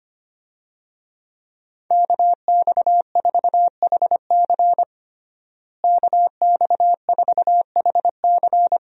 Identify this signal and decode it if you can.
Morse code, heard as a keyed tone, about 25 wpm, 700 Hz. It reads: KX4HC KX4HC
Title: USA Call Signs - 5 Characters - Repeated 2x - 25wpm